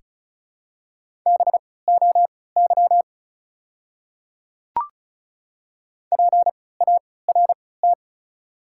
Morse code, heard as a keyed tone, about 34 wpm, spaced as for 20 wpm, 700 Hz. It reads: BOY PART
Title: 3x Character Spacing - Top 500 Words - 35wpm